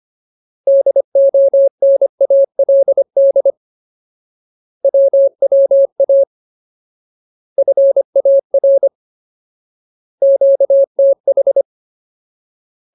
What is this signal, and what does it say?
Morse code, heard as a keyed tone, about 25 wpm, 550 Hz. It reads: DONALD WWA FAR QTH
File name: Ultimate Word List / 25WPM Ultimate